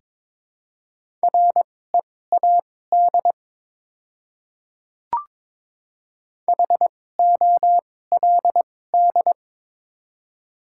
Morse code, heard as a keyed tone, about 23 wpm, spaced as for 16 wpm, 700 Hz. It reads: READ HOLD
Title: 2x Character Spacing - Top 500 Words - 22wpm